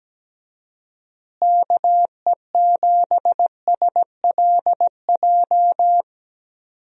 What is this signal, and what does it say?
Morse code, heard as a keyed tone, about 17 wpm, 700 Hz. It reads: KE7SLJ
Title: USA Call Signs - 6 Characters - Repeated 3x - 17wpm